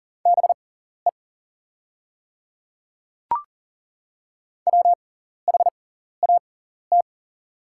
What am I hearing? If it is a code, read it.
Morse code, heard as a keyed tone, about 40 wpm, spaced as for 14 wpm, 700 Hz. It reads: BE WHAT
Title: 6x Character Spacing - Top 200 Words - 40wpm